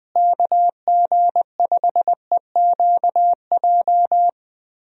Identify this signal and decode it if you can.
Morse code, heard as a keyed tone, about 20 wpm, 700 Hz. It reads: KG5EQJ
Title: USA Call Signs - 6 Characters - Repeated 3x - 20wpm